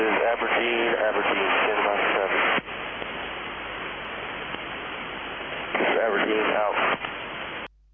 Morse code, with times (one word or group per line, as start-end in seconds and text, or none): none